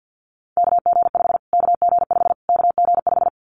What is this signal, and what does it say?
Morse code, heard as a keyed tone, about 50 wpm, 700 Hz. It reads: KG5 KG5 KG5
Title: Callsign Prefixes for Top 25 DXCC - Repeated 3x - 50wpm